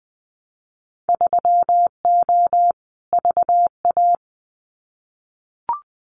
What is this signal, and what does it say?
Morse code, heard as a keyed tone, about 20 wpm, 700 Hz. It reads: 3O VA
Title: Winter Field Day Exchanges 20wpm